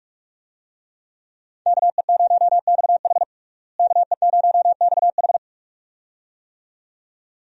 Morse code, heard as a keyed tone, about 46 wpm, 700 Hz. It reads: KE0XH KE0XH
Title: USA Call Signs - 5 Characters - Repeated 2x - 45wpm